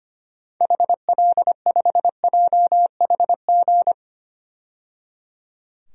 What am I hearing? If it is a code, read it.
Morse code, heard as a keyed tone, about 25 wpm, 700 Hz. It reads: HL5JHG
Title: DXCC-100 Call Signs - 6 Characters - 25wpm